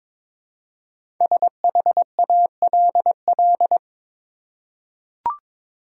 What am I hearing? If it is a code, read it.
Morse code, heard as a keyed tone, about 22 wpm, 700 Hz. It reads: SHALL
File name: 5 Letter Words - First 3 Letters Given - 22wpm